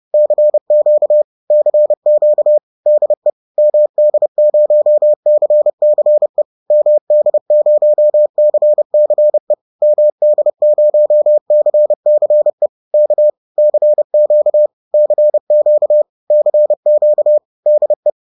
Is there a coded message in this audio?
Yes: Morse code from 0.1 to 18.1 s, heard as a keyed tone, about 30 wpm, 600 Hz. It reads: CQ CQ DE MD0CCE MD0CCE MD0CCE K CQ CQ CQ DE